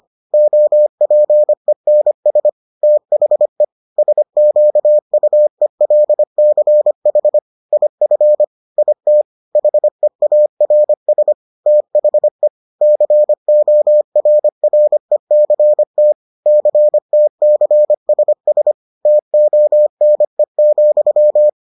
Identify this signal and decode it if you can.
Morse code, heard as a keyed tone, about 25 wpm, 600 Hz. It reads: OPENS THE SQUELCH IF IT HEARS THE CORRECT CTCSS TONE,